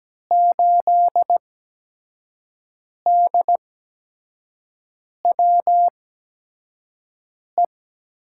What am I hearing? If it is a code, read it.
Morse code, heard as a keyed tone, about 17 wpm, spaced as for 5 wpm, 700 Hz. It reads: 8DWE
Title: USA Call Signs - 5 Characters - 8x Character Spacing - 17wpm